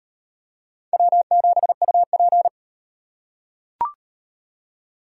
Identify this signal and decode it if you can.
Morse code, heard as a keyed tone, about 38 wpm, 700 Hz. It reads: W7UP